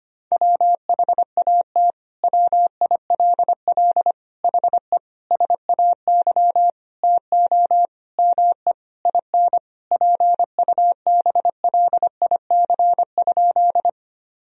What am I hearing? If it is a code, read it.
Morse code, heard as a keyed tone, about 25 wpm, 700 Hz. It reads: WHAT WILL HE SAY TO ME IN PUBLIC?
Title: Sentences from Top 1000 Words - Review Part 3 - 25wpm